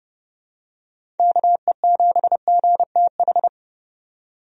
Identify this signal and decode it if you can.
Morse code, heard as a keyed tone, about 30 wpm, 700 Hz. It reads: KE7GTH
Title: USA Call Signs - 6 Characters - 30wpm